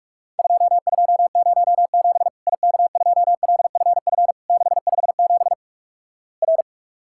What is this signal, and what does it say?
Morse code, heard as a keyed tone, about 46 wpm, 700 Hz. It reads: 2207 IK2LFF 657 R
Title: Worked All Europe DX Contest - QTC Exchange - 45wpm